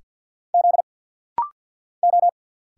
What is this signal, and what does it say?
Morse code, heard as a keyed tone, about 50 wpm, 700 Hz. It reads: Z O
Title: Single Letters - Rapid-Fire 50wpm